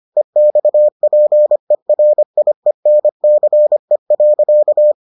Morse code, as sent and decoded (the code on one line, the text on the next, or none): . -..- .--. .-. .. -. -.-. .-.-.-
EXPERIENCE.